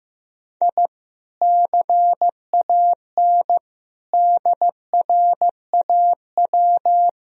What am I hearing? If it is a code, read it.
Morse code, heard as a keyed tone, about 15 wpm, 700 Hz. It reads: I CAN DRAW
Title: Sentences from Top 1000 Words - Review Part 1 - 15wpm